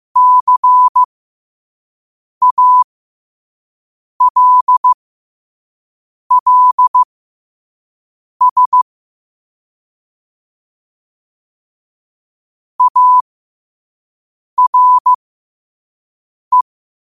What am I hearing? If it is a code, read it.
Morse code, heard as a keyed tone, about 15 wpm, spaced as for 5 wpm, 1000 Hz. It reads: CALLS ARE
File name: qotdfiles / 5wpm